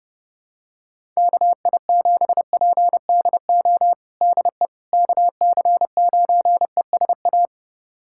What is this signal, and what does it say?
Morse code, heard as a keyed tone, about 32 wpm, 700 Hz. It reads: KI7PDO DE KC9ESA
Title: USA Callsign DE Callsign - 6 Characters - 30wpm